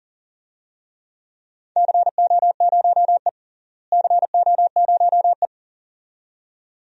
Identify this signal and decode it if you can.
Morse code, heard as a keyed tone, about 40 wpm, 700 Hz. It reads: CO0E CO0E
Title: DXCC-100 Call Signs - 4 Characters - Repeated 2x - 40wpm